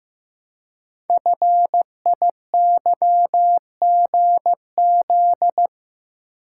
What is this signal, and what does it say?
Morse code, heard as a keyed tone, about 15 wpm, 700 Hz. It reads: FIYGZ
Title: Code Groups - 5 Letters - 15wpm